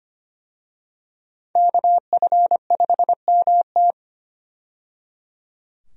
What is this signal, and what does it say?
Morse code, heard as a keyed tone, about 25 wpm, 700 Hz. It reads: KF5MT
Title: USA Call Signs - 5 Characters - 25wpm